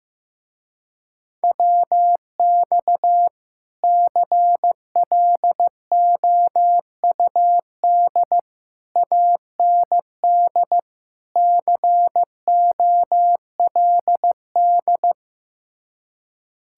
Morse code, heard as a keyed tone, about 15 wpm, 700 Hz. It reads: WX CLOUD AND COLD
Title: Speed-Racing Common QSO Phrases Encore - 15wpm